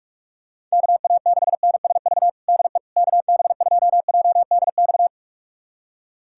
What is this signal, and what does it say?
Morse code, heard as a keyed tone, about 41 wpm, 700 Hz. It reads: KA6NSV DE KB2JDX
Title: USA Callsign DE Callsign 45wpm